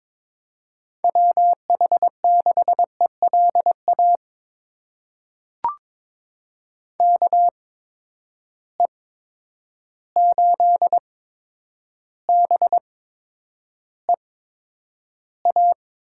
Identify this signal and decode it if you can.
Morse code, heard as a keyed tone, about 22 wpm, spaced as for 9 wpm, 700 Hz. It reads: WH6ELA KE8BEA
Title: USA Call Signs - 6 Characters - 8x Character Spacing - 22wpm